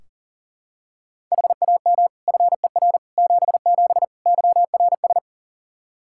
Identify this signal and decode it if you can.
Morse code, heard as a keyed tone, about 40 wpm, 700 Hz. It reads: HAM FER 77 YRS
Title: K1USN SST QSO Practice 40wpm